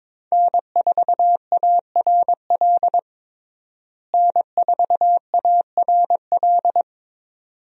Morse code, heard as a keyed tone, about 22 wpm, 700 Hz. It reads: N4ARL N4ARL
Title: USA Call Signs - 5 Characters - Repeated 2x - 22wpm